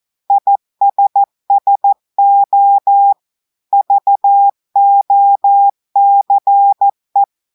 Morse code, heard as a keyed tone, about 14 wpm, 800 Hz. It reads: ISSO VOCE
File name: audios / portugais / portugais texte en clair 2 vitesse 14 mots minutes